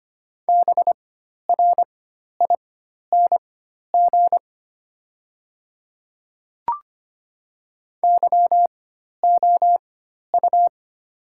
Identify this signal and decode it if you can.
Morse code, heard as a keyed tone, about 25 wpm, spaced as for 12 wpm, 700 Hz. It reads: BRING YOU
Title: 4x Character Spacing - Top 500 Words - 25wpm